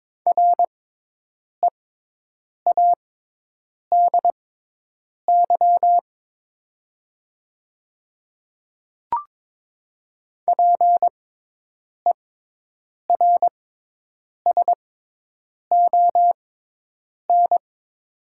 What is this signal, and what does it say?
Morse code, heard as a keyed tone, about 22 wpm, spaced as for 8 wpm, 700 Hz. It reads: READY PERSON
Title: 6x Character Spacing - Top 500 Words - 22wpm